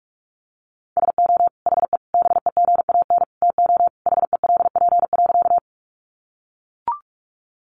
Morse code, heard as a keyed tone, about 44 wpm, 700 Hz. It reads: SO HE BEGAN TO HELP.
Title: Sentences from Top 300 Words 45wpm